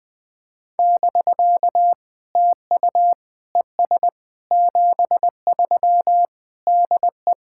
Morse code, heard as a keyed tone, about 20 wpm, 690 Hz. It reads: <BK> TU ES 73 DE
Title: POTA QSO Practice - 20wpm